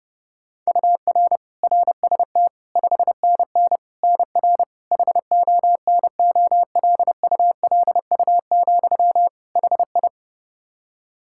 Morse code, heard as a keyed tone, about 30 wpm, 700 Hz. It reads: UR RST 5NN NR HONOLULU, HI
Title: Practice QSO - Exchange 1 - Beginning - 30wpm